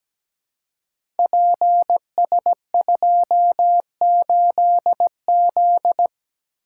Morse code, heard as a keyed tone, about 17 wpm, 700 Hz. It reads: PS28Z
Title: Code Groups - 5 Letter:Numbers - 17wpm